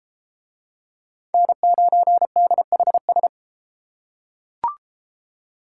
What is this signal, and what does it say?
Morse code, heard as a keyed tone, about 32 wpm, 700 Hz. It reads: N9DHS